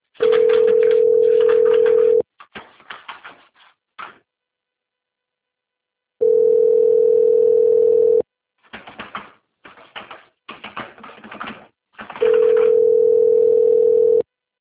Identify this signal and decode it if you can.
Ringback tone (North America), 440 + 480 Hz.
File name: rg-1600-unknown- - - .19674